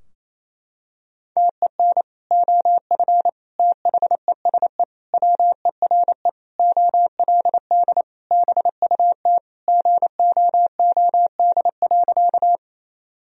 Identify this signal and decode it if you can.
Morse code, heard as a keyed tone, about 28 wpm, 700 Hz. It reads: TEN OF THESE WERE OLD BUT GOOD.
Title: Sentences from Top 1000 Words - Review Part 2 - 28wpm